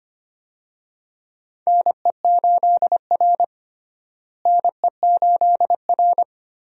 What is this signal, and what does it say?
Morse code, heard as a keyed tone, about 25 wpm, 700 Hz. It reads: NE8R NE8R
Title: USA Call Signs - 4 Characters - Repeated 3x - 25wpm